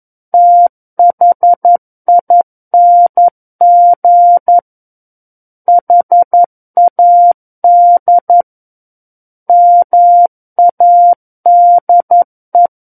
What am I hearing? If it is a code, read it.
Morse code, heard as a keyed tone, about 11 wpm, 700 Hz. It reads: THING HAD MADE
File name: 02-Chapter 2 - 11 WPM